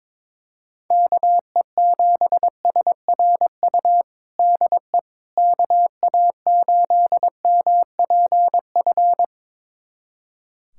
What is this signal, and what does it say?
Morse code, heard as a keyed tone, about 22 wpm, 700 Hz. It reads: KE7SRU DE KA8MPF
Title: USA Callsign DE Callsign 22wpm